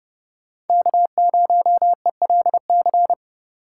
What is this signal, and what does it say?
Morse code, heard as a keyed tone, about 30 wpm, 700 Hz. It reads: K0ELC
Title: Speed Racing - USA Call Signs 30wpm